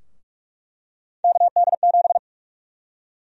Morse code, heard as a keyed tone, about 46 wpm, 700 Hz. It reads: KD7
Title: USA Call Sign Prefixes - 45wpm